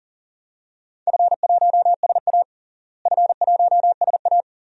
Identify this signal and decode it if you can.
Morse code, heard as a keyed tone, about 40 wpm, 690 Hz. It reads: F1SA F1SA
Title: DXCC-100 Call Signs - 4 Characters - Repeated 2x - 40wpm